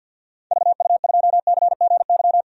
Morse code, heard as a keyed tone, about 51 wpm, 700 Hz. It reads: VU2CGY
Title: DXCC-100 Call Signs - 6 Characters - 50wpm